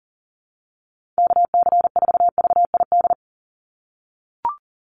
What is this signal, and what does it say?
Morse code, heard as a keyed tone, about 37 wpm, 700 Hz. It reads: KC4VID
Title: USA Call Signs 40wpm